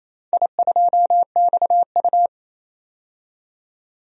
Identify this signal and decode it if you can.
Morse code, heard as a keyed tone, about 28 wpm, 700 Hz. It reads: I2XU
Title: DXCC-100 Call Signs - 4 Characters - 28wpm